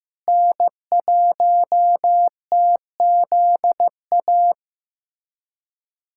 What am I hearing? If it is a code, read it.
Morse code, heard as a keyed tone, about 16 wpm, 700 Hz. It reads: N1TZA
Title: USA Callsign DE Callsign - 5 Characters - 15wpm